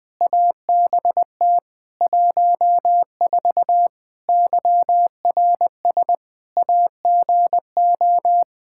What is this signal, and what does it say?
Morse code, heard as a keyed tone, about 20 wpm, 700 Hz. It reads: ABT 14 YRS AGO
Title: Practice QSO - Exchange 3 - More Info - 20wpm